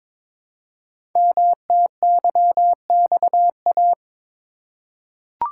Morse code, heard as a keyed tone, about 22 wpm, 700 Hz. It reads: MTYXA
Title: Code Groups - 5 Letters - 22wpm